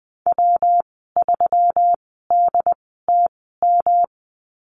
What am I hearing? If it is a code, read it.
Morse code, heard as a keyed tone, about 20 wpm, spaced as for 15 wpm, 700 Hz. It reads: W3DTM
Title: 161 Lesson 26g - Letter g - Call Signs 20wpm